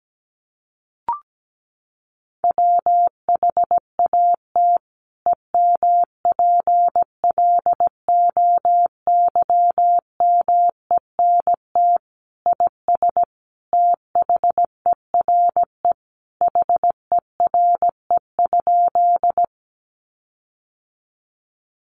Morse code, heard as a keyed tone, about 17 wpm, 700 Hz. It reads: WHAT EMPLOYMENT IS THERE HERE?
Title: Sentences from Top 1500 Words 17wpm